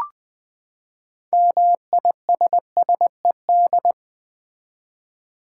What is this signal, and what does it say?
Morse code, heard as a keyed tone, about 21 wpm, 700 Hz. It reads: MISSED
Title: Common QSO Phrases - 4x Word-Spacing - 20wpm